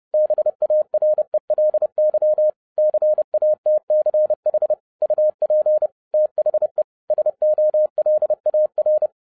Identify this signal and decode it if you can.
Morse code, heard as a keyed tone, about 30 wpm, 610 Hz. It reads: BARELY CATCH UP THE SOLAR
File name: 2012-01-mp3 / n11-I DIDNT MISS THE SOLAR CYCLE 24-30wpm